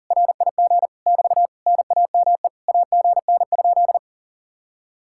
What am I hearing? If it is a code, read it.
Morse code, heard as a keyed tone, about 50 wpm, 700 Hz. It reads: RIG <BT> NAME AGN?